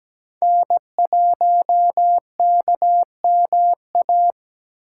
Morse code, heard as a keyed tone, about 20 wpm, 700 Hz. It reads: N1KMA
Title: Speed Racing - USA Call Signs 17wpm